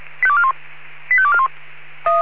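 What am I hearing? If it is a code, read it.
Five-tone selective call (ZVEI), heard as quick runs of five tones, tones of about 70 ms. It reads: 85241
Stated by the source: ZVEI five-tone selective call 85241, received on SDR.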